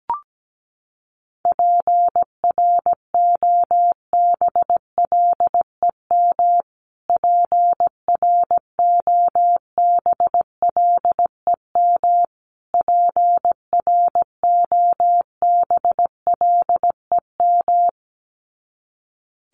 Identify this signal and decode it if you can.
Morse code, heard as a keyed tone, about 17 wpm, 700 Hz. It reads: PROBLEM PROBLEM PROBLEM
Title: Top 500 Words Repeated 17wpm